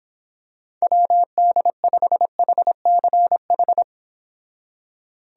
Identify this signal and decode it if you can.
Morse code, heard as a keyed tone, about 26 wpm, 700 Hz. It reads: WD5HCH
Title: Speed Racing - USA 2x3 Call Signs 22wpm